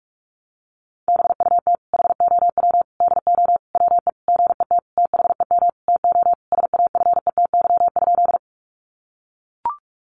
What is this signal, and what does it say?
Morse code, heard as a keyed tone, about 44 wpm, 700 Hz. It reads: BUT HOW DO WE GET THEM TO SAFETY?
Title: Sentences from Top 700 Words 45wpm